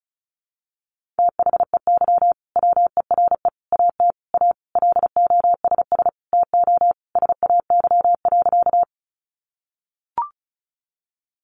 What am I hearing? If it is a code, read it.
Morse code, heard as a keyed tone, about 34 wpm, 700 Hz. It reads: THEY WERE AT A LOSS TO SAY.
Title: Sentences from Top 1400 Words 35wpm